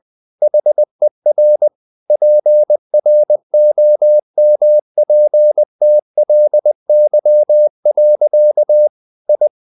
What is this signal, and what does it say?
Morse code, heard as a keyed tone, about 20 wpm, 600 Hz. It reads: HER PROMPTLY. I
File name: the old man and the sea / the old man and the sea part 1 20wpm